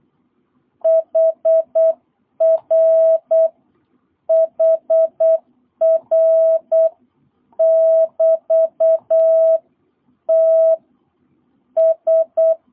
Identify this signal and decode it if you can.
Morse code, heard as a keyed tone, about 8 wpm, 650 Hz. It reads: HR HR <BT> T S
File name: jh8cbh / 2022cw